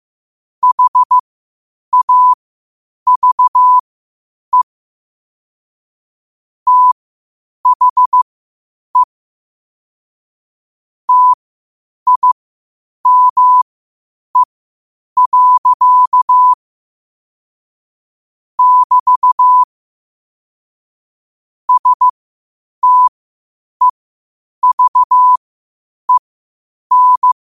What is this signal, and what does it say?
Morse code, heard as a keyed tone, about 15 wpm, spaced as for 8 wpm, 1000 Hz. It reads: HAVE THE TIME. <BT> STEVEN
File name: qotdfiles / 8wpm